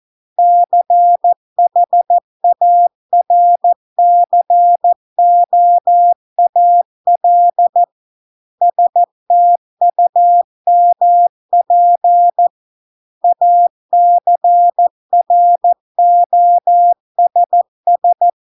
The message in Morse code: CHARCOAL STUMP ACROSS

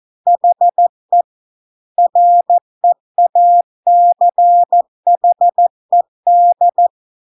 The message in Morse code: HE REACHED

